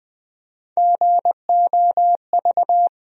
Morse code, heard as a keyed tone, about 20 wpm, 700 Hz. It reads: GOV
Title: USA Call Sign Postfixes - 3 Letters 20wpm